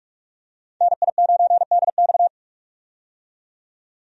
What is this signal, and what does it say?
Morse code, heard as a keyed tone, about 44 wpm, 700 Hz. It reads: NI9DX
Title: USA Call Signs - 5 Characters - 45wpm